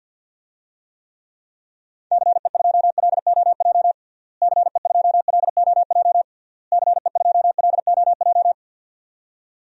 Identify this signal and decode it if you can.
Morse code, heard as a keyed tone, about 52 wpm, 700 Hz. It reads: CE2LOJ CE2LOJ CE2LOJ
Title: DXCC-100 Call Signs - 6 Characters - Repeated 3x - 50wpm